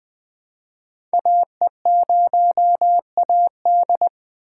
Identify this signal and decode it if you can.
Morse code, heard as a keyed tone, about 20 wpm, 700 Hz. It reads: AE0AD
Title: Code Groups - 5 Letter:Numbers - 20wpm